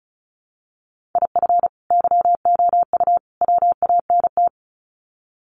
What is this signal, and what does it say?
Morse code, heard as a keyed tone, about 34 wpm, 700 Hz. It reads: IF YOU WANT
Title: Sets of 3 Words 35wpm